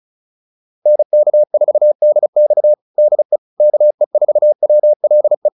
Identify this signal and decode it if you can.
Morse code, heard as a keyed tone, about 34 wpm, 600 Hz. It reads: NK4DX DE KE4WLE